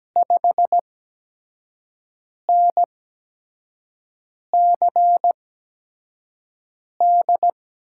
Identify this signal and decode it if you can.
Morse code, heard as a keyed tone, about 17 wpm, 700 Hz. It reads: 5 N C D
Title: USA Call Signs - 6 Characters - 8x Character Spacing - 17wpm